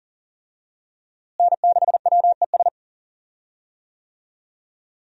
Morse code, heard as a keyed tone, about 40 wpm, 690 Hz. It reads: N6WES
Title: USA Call Signs - 5 Characters - Repeated 3x - 40wpm